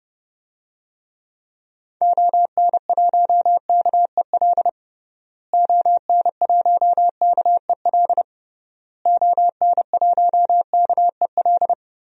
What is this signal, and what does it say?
Morse code, heard as a keyed tone, about 30 wpm, 700 Hz. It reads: ON1KEL ON1KEL ON1KEL
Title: DXCC-100 Call Signs - 6 Characters - Repeated 3x - 30wpm